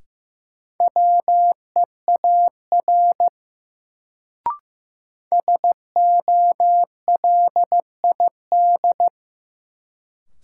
Morse code, heard as a keyed tone, about 15 wpm, 700 Hz. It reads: WEAR SOLID
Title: Sentences from Top 1500 Words - New Word Drill Alt - 15wpm